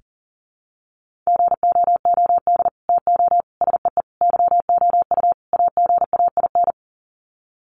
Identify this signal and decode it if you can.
Morse code, heard as a keyed tone, about 39 wpm, 700 Hz. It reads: GOOD TO SEE YOU AGAIN
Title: Sets of 5 Words Encore2 40wpm